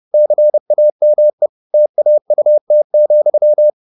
Morse code, heard as a keyed tone, about 30 wpm, 600 Hz. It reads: CAME TAUT,